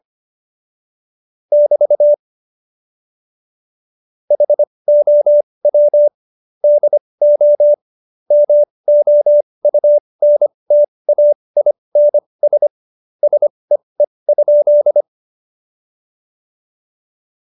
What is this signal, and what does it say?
Morse code, heard as a keyed tone, about 25 wpm, spaced as for 20 wpm, 590 Hz. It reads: <BT> HOW DO MOUNTAINS SEE?
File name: Jokes-25-char-20-eff